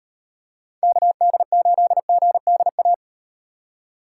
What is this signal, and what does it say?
Morse code, heard as a keyed tone, about 36 wpm, 700 Hz. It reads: KD8GDA